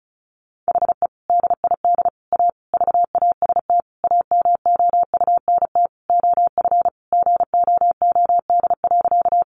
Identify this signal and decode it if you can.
Morse code, heard as a keyed tone, about 34 wpm, 700 Hz. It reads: HE DID A VAST AMOUNT OF GOOD.
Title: Sentences from Top 1300 Words 35wpm